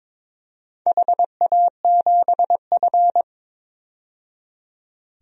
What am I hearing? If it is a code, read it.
Morse code, heard as a keyed tone, about 22 wpm, 690 Hz. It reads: HA7F
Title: DXCC-100 Call Signs - 4 Characters - 22wpm